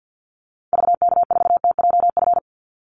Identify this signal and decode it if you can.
Morse code, heard as a keyed tone, about 48 wpm, 700 Hz. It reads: VK4TJF
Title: Call Signs Repeated 3x - 50wpm